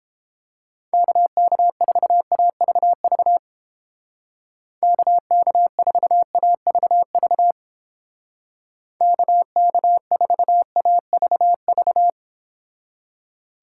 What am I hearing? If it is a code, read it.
Morse code, heard as a keyed tone, about 31 wpm, 700 Hz. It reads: KK4AVV KK4AVV KK4AVV